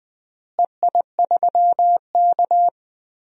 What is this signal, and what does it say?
Morse code, heard as a keyed tone, about 21 wpm, 700 Hz. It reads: EI3K
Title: DXCC-100 Call Signs - 4 Characters - Repeated 2x - 20wpm